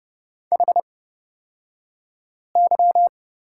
Morse code, heard as a keyed tone, about 30 wpm, spaced as for 5 wpm, 700 Hz. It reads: HY
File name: Lesson 37d - Letter x - Two Characters 30wpm